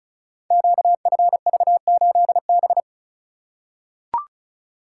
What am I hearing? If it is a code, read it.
Morse code, heard as a keyed tone, about 36 wpm, 700 Hz. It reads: QFV8B
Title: Code Groups - 5 Letter:Numbers - 35wpm